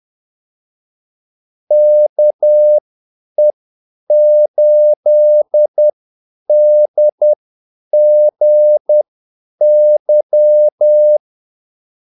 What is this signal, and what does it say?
Morse code, heard as a keyed tone, about 10 wpm, spaced as for 8 wpm, 600 Hz. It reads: KE8DGY
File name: Callsigns-10-8wpm